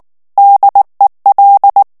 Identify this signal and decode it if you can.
Morse code, heard as a keyed tone, about 19 wpm, 800 Hz. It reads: DEL
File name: morse mp3 / ESPAGNOL TEXTE EN CLAIR 2 VITESSE 18 MOTS MINUTES